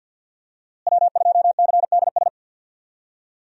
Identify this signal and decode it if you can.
Morse code, heard as a keyed tone, about 50 wpm, 700 Hz. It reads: W2CDS
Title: USA Call Signs - 5 Characters - Repeated 3x - 50wpm